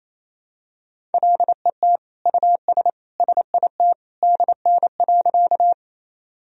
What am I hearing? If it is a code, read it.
Morse code, heard as a keyed tone, about 28 wpm, 700 Hz. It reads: LET US SIT DN.
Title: Sentences from Top 500 Words and Abbreviations 28wpm